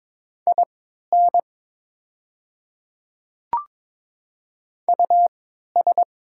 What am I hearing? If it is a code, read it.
Morse code, heard as a keyed tone, about 22 wpm, spaced as for 12 wpm, 690 Hz. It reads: IN US
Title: 3x Character Spacing - Top 300 Words - 22wpm